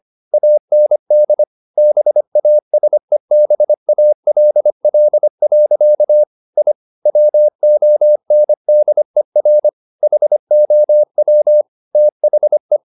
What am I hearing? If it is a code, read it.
Morse code, heard as a keyed tone, about 25 wpm, 600 Hz. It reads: AND BASEBALL. I WONDER HOW THE